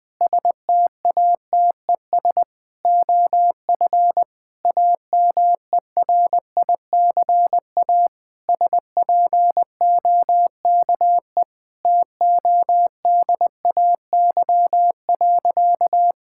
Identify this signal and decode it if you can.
Morse code, heard as a keyed tone, about 21 wpm, 700 Hz. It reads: STATES OF AMERICA SPOKE TODAY.
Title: Sentences with 8-Word Phrases - 20wpm